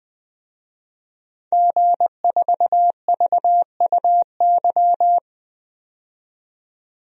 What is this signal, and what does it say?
Morse code, heard as a keyed tone, about 20 wpm, 700 Hz. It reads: G4VUY
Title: DXCC-100 Call Signs - 5 Characters - 20wpm